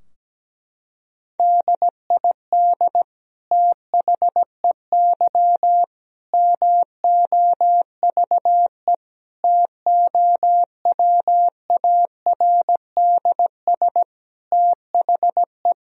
Morse code, heard as a keyed tone, about 17 wpm, 700 Hz. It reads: DID THEY MOVE TOWARDS THE
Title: Sentences from Top 1000 Words - Review Part 2 - 17wpm